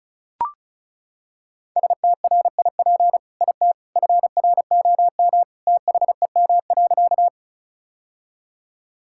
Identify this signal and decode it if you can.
Morse code, heard as a keyed tone, about 36 wpm, 700 Hz. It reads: STRIP IT FROM THEM.
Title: Sentences from Top 1600 Words 35wpm